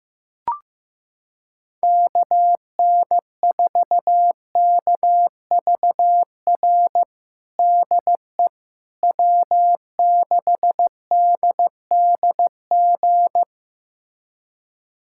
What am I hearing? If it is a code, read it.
Morse code, heard as a keyed tone, about 15 wpm, 700 Hz. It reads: KN4KVR DE W6DDG
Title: USA Callsign DE Callsign 15wpm